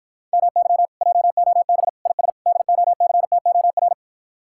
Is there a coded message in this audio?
Yes: Morse code from 0.3 to 3.9 s, heard as a keyed tone, about 60 wpm, 700 Hz. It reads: MY JOB IS DOCTOR